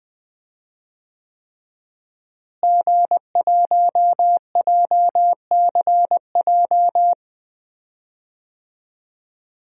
Morse code, heard as a keyed tone, about 20 wpm, 700 Hz. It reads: G1JCJ
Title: DXCC-100 Call Signs - 5 Characters - Repeated 3x - 20wpm